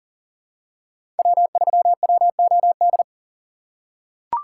Morse code, heard as a keyed tone, about 40 wpm, 700 Hz. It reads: W3WOD
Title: USA Call Signs - 5 Characters - 3x Character Spacing - 40wpm